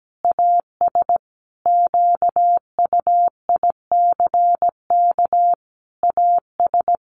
Morse code, heard as a keyed tone, about 17 wpm, 700 Hz. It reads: AS QUICK AS
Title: Sentences from Top 500 Words 17wpm